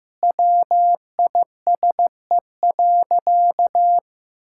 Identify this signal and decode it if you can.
Morse code, heard as a keyed tone, about 15 wpm, 700 Hz. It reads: WISE.
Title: Sentences from Top 1600 Words 15wpm